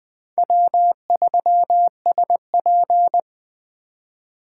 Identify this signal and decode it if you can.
Morse code, heard as a keyed tone, about 20 wpm, 700 Hz. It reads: W3SP
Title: Speed Racing - USA 1x2 Call Signs 17wpm